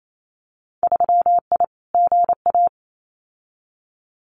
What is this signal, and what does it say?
Morse code, heard as a keyed tone, about 28 wpm, 700 Hz. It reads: 3I GA
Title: Winter Field Day Exchanges 28wpm